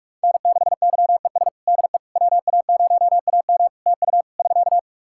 Morse code, heard as a keyed tone, about 48 wpm, 700 Hz. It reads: N6YES DE WA0AM TU <SK>
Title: Practice QSO - Exchange 4 - Ending - 45wpm